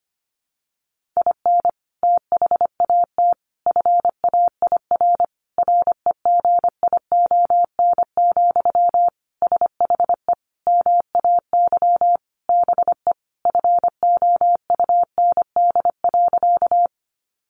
Morse code, heard as a keyed tone, about 25 wpm, 700 Hz. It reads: IN THAT FAIR REGION, SHE MAY BE FOUND.
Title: Sentences from Top 1500 Words 25wpm